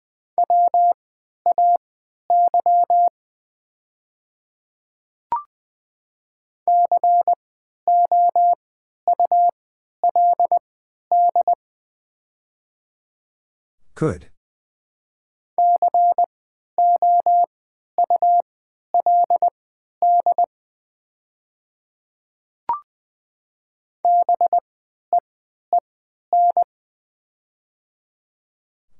Morse code, heard as a keyed tone, about 20 wpm, spaced as for 11 wpm, 700 Hz. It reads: WAY COULD COULD BEEN
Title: 3x Character Spacing - Top 100 Words - 20wpm